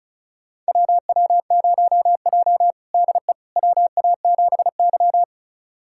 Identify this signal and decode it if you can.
Morse code, heard as a keyed tone, about 36 wpm, 700 Hz. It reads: WW0J DE WA7Y
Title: USA Callsign DE Callsign - 4 Characters - 35wpm